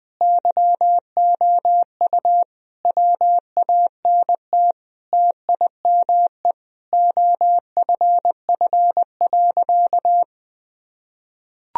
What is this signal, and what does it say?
Morse code, heard as a keyed tone, about 21 wpm, 700 Hz. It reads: YOU WANT TIME OFF.
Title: Sentences from Top 1000 Words - Review Part 3 - 20wpm